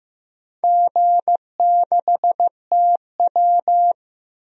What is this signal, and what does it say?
Morse code, heard as a keyed tone, about 15 wpm, 700 Hz. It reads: G6TW
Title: DXCC-100 Call Signs - 4 Characters - 15wpm